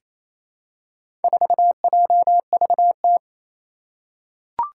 Morse code, heard as a keyed tone, about 28 wpm, 700 Hz. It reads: 4JVT
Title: Four Challenging Letter-Number Combinations 28wpm